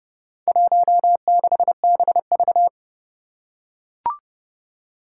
Morse code, heard as a keyed tone, about 32 wpm, 700 Hz. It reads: 16BV
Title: Four Challenging Letter-Number Combinations 30wpm